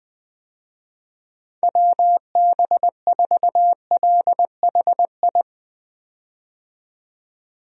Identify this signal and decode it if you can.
Morse code, heard as a keyed tone, about 20 wpm, 700 Hz. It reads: WB4LHI
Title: USA Call Signs - 6 Characters - 20wpm